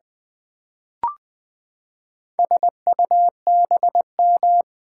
Morse code, heard as a keyed tone, about 20 wpm, 700 Hz. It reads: SUBM
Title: Build A Word - 4 Letters At A Time - 10-Letter Words - 20wpm